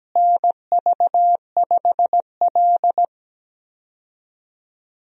Morse code, heard as a keyed tone, about 17 wpm, 700 Hz. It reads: NV5L
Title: USA Call Signs - 4 Characters - Repeated 3x - 17wpm